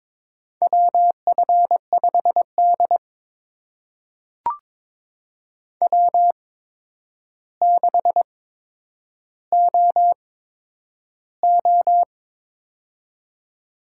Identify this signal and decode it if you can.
Morse code, heard as a keyed tone, about 21 wpm, spaced as for 9 wpm, 700 Hz. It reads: WF5D W6OO
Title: USA Call Signs - 4 Characters - 8x Character Spacing - 22wpm